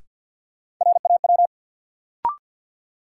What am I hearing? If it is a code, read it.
Morse code, heard as a keyed tone, about 53 wpm, 700 Hz. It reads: RAW
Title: Sentences from Top 1500 Words - New Word Drill Alt - 50wpm